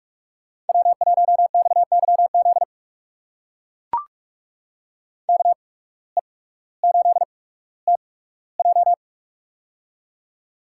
Morse code, heard as a keyed tone, about 44 wpm, spaced as for 19 wpm, 690 Hz. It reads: W1XYZ KE8TJ
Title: USA Call Signs - 5 Characters - 8x Character Spacing - 45wpm